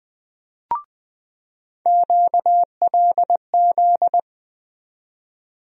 Morse code, heard as a keyed tone, about 20 wpm, 700 Hz. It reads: QLZ
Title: USA Call Sign Postfixes - 3 Letters 20wpm